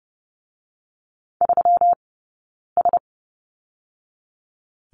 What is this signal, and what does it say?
Morse code, heard as a keyed tone, about 30 wpm, 700 Hz. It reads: 3 S
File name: Lesson 20d - Number 3 - Two Characters 30wpm